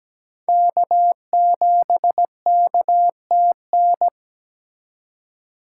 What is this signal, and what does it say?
Morse code, heard as a keyed tone, about 20 wpm, 700 Hz. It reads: K7KTN
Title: Speed Racing - USA Call Signs 17wpm